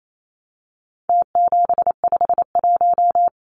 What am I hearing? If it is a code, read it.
Morse code, heard as a keyed tone, about 28 wpm, 700 Hz. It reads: T751
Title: Numbers - Contest Style - 4-Digit 28wpm